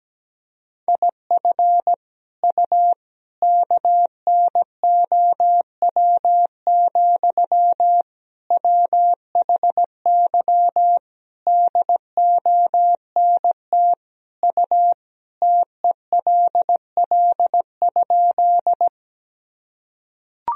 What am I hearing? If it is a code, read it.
Morse code, heard as a keyed tone, about 17 wpm, 700 Hz. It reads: IF U KNOW, WHY DONT U TELL?
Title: Sentences from Top 300 Words and Abbreviations 17wpm